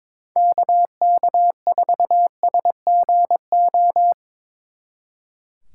Morse code, heard as a keyed tone, about 25 wpm, 700 Hz. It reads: KK4SGO
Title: Speed Racing - USA 2x3 Call Signs 22wpm